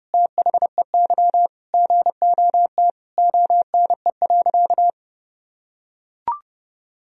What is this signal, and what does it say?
Morse code, heard as a keyed tone, about 32 wpm, 700 Hz. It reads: THEY GOT ONE.
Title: Sentences from Top 300 Words and Abbreviations 30wpm